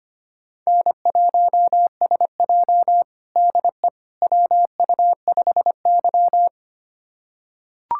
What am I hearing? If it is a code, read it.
Morse code, heard as a keyed tone, about 25 wpm, 700 Hz. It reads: N1SJ DE WU5Y
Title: USA Callsign DE Callsign - 4 Characters - 25wpm